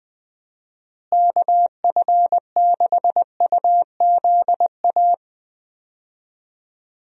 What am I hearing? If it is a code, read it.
Morse code, heard as a keyed tone, about 21 wpm, 700 Hz. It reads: KF6UZA
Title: USA Call Signs - 6 Characters - Repeated 3x - 20wpm